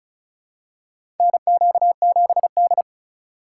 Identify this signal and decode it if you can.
Morse code, heard as a keyed tone, about 36 wpm, 700 Hz. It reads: NQ7D
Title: USA Call Signs - 4 Characters - 3x Character Spacing - 35wpm